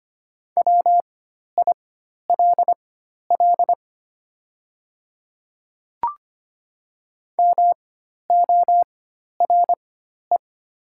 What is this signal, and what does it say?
Morse code, heard as a keyed tone, about 25 wpm, spaced as for 12 wpm, 700 Hz. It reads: WILL MORE
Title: 4x Character Spacing - Top 200 Words - 25wpm